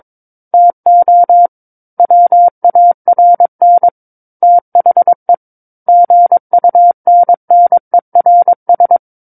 Morse code, heard as a keyed tone, about 22 wpm, 700 Hz. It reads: TO WARN THE GUNNERS
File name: Book One / 13-Chapter 13 - 22 WPM